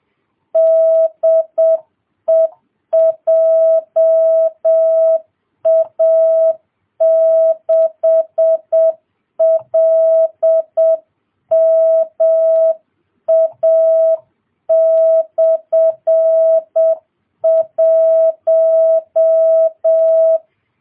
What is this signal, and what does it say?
Morse code, heard as a keyed tone, about 7 wpm, 650 Hz. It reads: DE JA6LMA/1